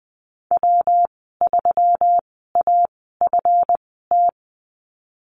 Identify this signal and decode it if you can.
Morse code, heard as a keyed tone, about 20 wpm, spaced as for 15 wpm, 700 Hz. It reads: W3AFT
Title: 167 Lesson 27g - Number 7 - Call Signs 20wpm